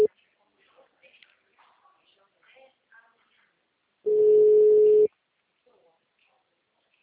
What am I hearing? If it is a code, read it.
Ringback tone (Europe, 425 Hz), with speech.